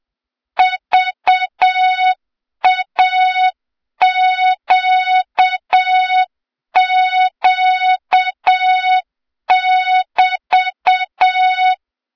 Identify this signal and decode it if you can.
Morse code, heard as a keyed tone, about 7 wpm, 3700 Hz, with noise. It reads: VAQQ<BT>
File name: Morze-HA2MN / hang / 23 GYAKORLAT